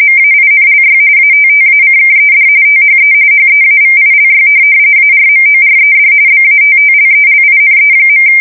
RTTY, about 45.45 baud. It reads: MAINLY ATTACKED THE HEAVILY ARMORED BATTLELINE.
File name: fsk170 / USS-New-Jersey